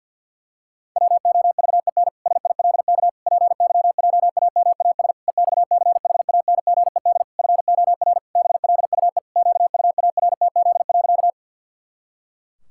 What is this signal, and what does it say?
Morse code, heard as a keyed tone, about 50 wpm, 700 Hz. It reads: WOVEN SILK PYJAMAS EXCHANGED FOR BLUE QUARTZ.
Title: Pangram Challenge v2 - 50wpm